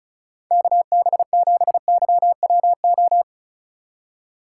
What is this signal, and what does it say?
Morse code, heard as a keyed tone, about 34 wpm, 700 Hz. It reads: KB7YWO
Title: USA Call Signs - 2x Character Spacing - 35wpm